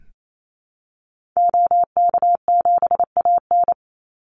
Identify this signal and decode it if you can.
Morse code, heard as a keyed tone, about 28 wpm, 700 Hz. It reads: OK7AN
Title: Call Signs Repeated 3x - 28wpm